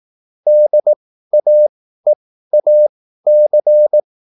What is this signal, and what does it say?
Morse code, heard as a keyed tone, about 18 wpm, spaced as for 13 wpm, 600 Hz. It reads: DAEAC
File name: QSL / CW lesson